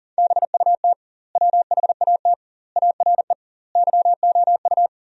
Morse code, heard as a keyed tone, about 40 wpm, 700 Hz. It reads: BUT WHAT ARE YOU
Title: 2x Word-Spacing - Sentences from Top 100 Words - 40wpm